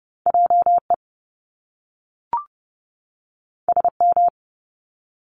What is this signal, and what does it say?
Morse code, heard as a keyed tone, about 30 wpm, 700 Hz. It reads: JE SM
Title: Top Two Letter Combinations in Words 30wpm